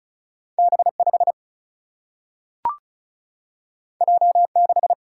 Morse code, heard as a keyed tone, about 36 wpm, 700 Hz. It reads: B5 J6
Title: Single-Letter Single-Number 35wpm